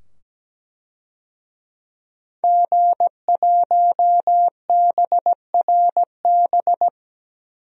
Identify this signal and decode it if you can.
Morse code, heard as a keyed tone, about 17 wpm, 700 Hz. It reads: G1BRB
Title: DXCC-100 Call Signs - 5 Characters - 17wpm